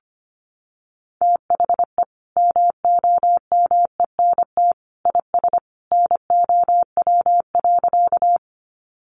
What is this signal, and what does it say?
Morse code, heard as a keyed tone, about 25 wpm, 700 Hz. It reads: THE MOMENT IS NOW.
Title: Sentences from Top 1000 Words 25wpm